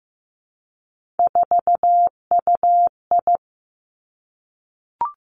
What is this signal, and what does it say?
Morse code, heard as a keyed tone, about 15 wpm, 700 Hz. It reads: 4UI
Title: Three Challenging Letter-Number Combinations 15wpm